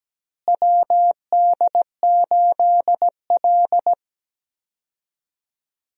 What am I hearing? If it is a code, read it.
Morse code, heard as a keyed tone, about 17 wpm, 700 Hz. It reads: WD8L
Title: USA Call Signs - 4 Characters - Repeated 2x - 17wpm